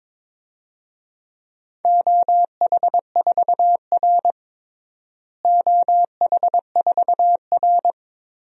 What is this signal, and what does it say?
Morse code, heard as a keyed tone, about 22 wpm, 700 Hz. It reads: OH4R OH4R
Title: DXCC-100 Call Signs - 4 Characters - Repeated 3x - 22wpm